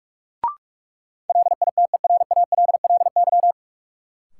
Morse code, heard as a keyed tone, about 46 wpm, 700 Hz. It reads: LITERALLY
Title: Sentences from Top 1400 Words - New Word Drill - 45wpm